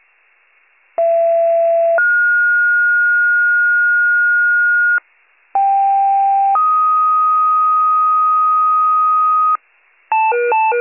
Two-tone page, heard as a one-second tone, then a three-second tone, about 670 then 1430 Hz, about 790 then 1250 Hz, with noise. It is heard from 1.0 to 9.6 s.